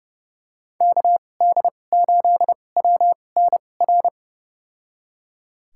Morse code, heard as a keyed tone, about 30 wpm, spaced as for 22 wpm, 700 Hz. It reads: KD8WNR
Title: USA Call Signs - 6 Characters - 2x Character Spacing - 30wpm